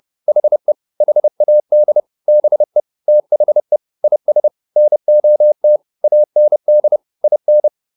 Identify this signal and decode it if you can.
Morse code, heard as a keyed tone, about 30 wpm, 600 Hz. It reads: HE HAD BE THE IS NOT AND IN